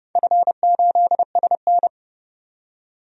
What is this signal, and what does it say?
Morse code, heard as a keyed tone, about 30 wpm, 700 Hz. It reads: F8SN
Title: DXCC-100 Call Signs - 4 Characters - 30wpm